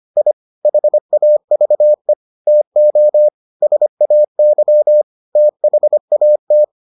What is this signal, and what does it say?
Morse code, heard as a keyed tone, about 25 wpm, 600 Hz. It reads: I HAVE TO SAY THAT